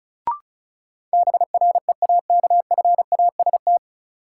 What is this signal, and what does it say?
Morse code, heard as a keyed tone, about 36 wpm, 700 Hz. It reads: BREAKFAST
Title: Sentences from Top 1500 Words - New Word Drill - 35wpm